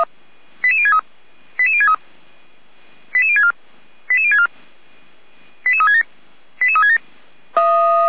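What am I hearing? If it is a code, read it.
Five-tone selective call (ZVEI), heard as quick runs of five tones, tones of about 70 ms. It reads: 88063 88063 88064 88064 88367 88367
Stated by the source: ZVEI five-tone selective call received by SDR.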